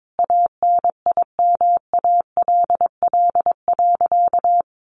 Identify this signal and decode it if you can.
Morse code, heard as a keyed tone, about 22 wpm, 700 Hz. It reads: ANIMALL.
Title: A Sound Of Thunder 22wpm